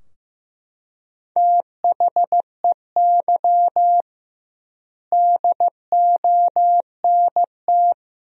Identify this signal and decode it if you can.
Morse code, heard as a keyed tone, about 15 wpm, 700 Hz. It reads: THEY DONT
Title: 2x Word-Spacing - Sentences from Top 300 Words - 15wpm